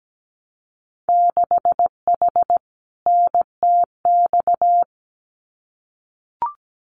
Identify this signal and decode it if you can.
Morse code, heard as a keyed tone, about 17 wpm, 700 Hz. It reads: 6H NTX
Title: Winter Field Day Exchanges Encore 17wpm